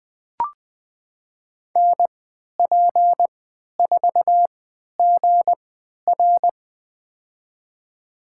Morse code, heard as a keyed tone, about 20 wpm, spaced as for 11 wpm, 700 Hz. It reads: NP4GR
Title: USA Call Signs - 3x Character Spacing - 20wpm